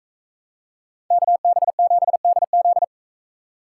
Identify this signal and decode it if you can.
Morse code, heard as a keyed tone, about 39 wpm, 700 Hz. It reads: KB7DZ